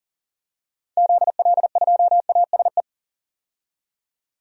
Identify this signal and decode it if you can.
Morse code, heard as a keyed tone, about 40 wpm, 700 Hz. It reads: ZL2ASE
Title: DXCC-100 Call Signs - 6 Characters - 40wpm